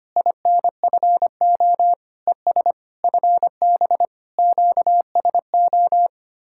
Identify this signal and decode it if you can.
Morse code, heard as a keyed tone, about 24 wpm, 700 Hz. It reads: INFO ES FB QSO
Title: Speed-Racing Common QSO Phrases - 25wpm